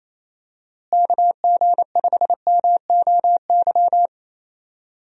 Morse code, heard as a keyed tone, about 27 wpm, 700 Hz. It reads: KG5MOY
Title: USA Call Signs - 6 Characters - Repeated 2x - 28wpm